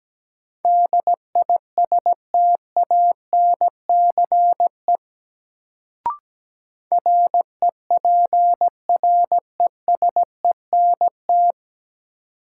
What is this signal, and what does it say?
Morse code, heard as a keyed tone, about 17 wpm, 700 Hz. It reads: DISTANCE REPRESENT
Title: Sentences from Top 1200 Words - New Word Drill Alt - 17wpm